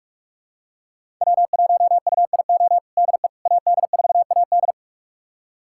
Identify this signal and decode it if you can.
Morse code, heard as a keyed tone, about 47 wpm, 700 Hz. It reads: W1UIO DE AD4AD
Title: USA Callsign DE Callsign - 5 Characters - 45wpm